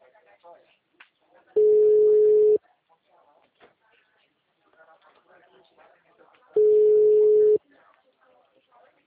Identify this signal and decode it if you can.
Ringback tone (Europe, 425 Hz), with speech.